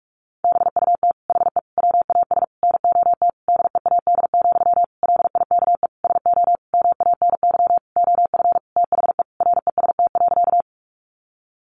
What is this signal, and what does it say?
Morse code, heard as a keyed tone, about 44 wpm, 700 Hz. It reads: BUT HE WAS NOT DEAD, LIKE SO MANY OF THE REST.
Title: Sentences from Top 1500 Words 45wpm